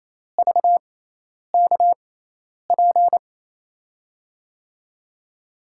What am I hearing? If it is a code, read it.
Morse code, heard as a keyed tone, about 28 wpm, spaced as for 10 wpm, 700 Hz. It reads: VKP
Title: USA Call Signs - 6 Characters - 6x Character Spacing - 28wpm